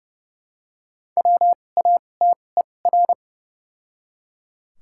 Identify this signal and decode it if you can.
Morse code, heard as a keyed tone, about 32 wpm, spaced as for 22 wpm, 700 Hz. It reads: WATER
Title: 2x Character Spacing - Top 100 Words - 30wpm